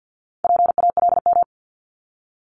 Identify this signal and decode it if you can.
Morse code, heard as a keyed tone, about 50 wpm, 700 Hz. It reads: PALM
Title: Multi-letter Phonemes 50wpm